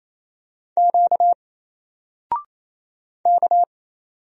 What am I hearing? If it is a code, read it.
Morse code, heard as a keyed tone, about 28 wpm, 700 Hz. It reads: Q K
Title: Single Letter-Number 28wpm